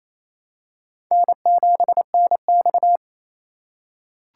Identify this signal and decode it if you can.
Morse code, heard as a keyed tone, about 28 wpm, 700 Hz. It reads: N7NX
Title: USA Call Signs - 4x Character Spacing - 28wpm